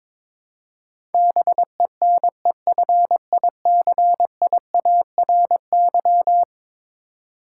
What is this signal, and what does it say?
Morse code, heard as a keyed tone, about 22 wpm, 700 Hz. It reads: BENEFICIARY
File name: practice / 22wpm / Eleven Letter Words - 22wpm